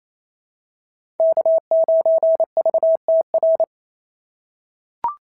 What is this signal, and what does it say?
Morse code, heard as a keyed tone, about 27 wpm, 690 Hz. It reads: K9VTR
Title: SOTA QSO Practice 28wpm